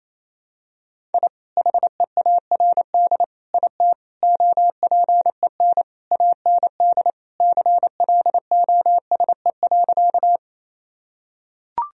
Morse code, heard as a keyed tone, about 28 wpm, 700 Hz. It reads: I HEARD IT OPEN AND CLOSE.
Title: Sentences from Top 1000 Words - Review Part 1 - 28wpm